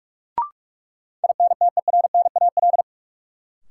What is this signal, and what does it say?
Morse code, heard as a keyed tone, about 46 wpm, 700 Hz. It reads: INTERNAL
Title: Sentences from Top 1400 Words - New Word Drill Alt - 45wpm